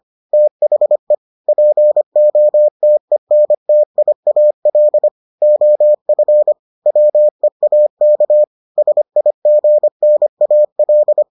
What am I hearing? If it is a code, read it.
Morse code, heard as a keyed tone, about 25 wpm, 600 Hz. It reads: THE POTENTIAL OF WEAK SIGNAL